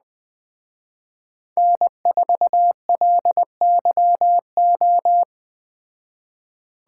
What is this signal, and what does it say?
Morse code, heard as a keyed tone, about 20 wpm, 700 Hz. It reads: N4LYO
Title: Speed Racing - USA 1x3 Call Signs 17wpm